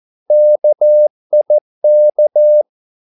Morse code, heard as a keyed tone, about 14 wpm, 600 Hz. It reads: KIK
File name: Korte ord DK 14wpm